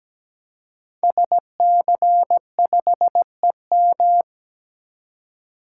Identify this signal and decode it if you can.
Morse code, heard as a keyed tone, about 17 wpm, 700 Hz. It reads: SC5EM
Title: Code Groups - 5 Letter:Numbers - 17wpm